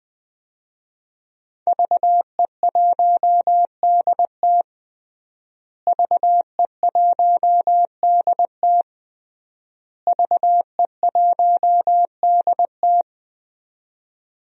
Morse code, heard as a keyed tone, about 20 wpm, 700 Hz. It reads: VE1DT VE1DT VE1DT
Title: DXCC-100 Call Signs - 5 Characters - Repeated 3x - 20wpm